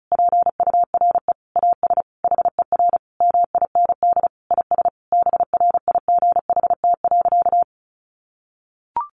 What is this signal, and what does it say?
Morse code, heard as a keyed tone, about 34 wpm, 700 Hz. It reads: PURE AS HER MIND IS BRIGHT.
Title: Sentences from Top 1500 Words 35wpm